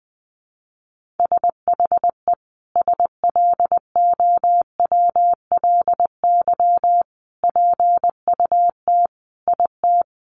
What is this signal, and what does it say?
Morse code, heard as a keyed tone, about 20 wpm, 700 Hz. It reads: SHE SLOWLY PUT IT
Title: Sentences from Top 1100 Words 20wpm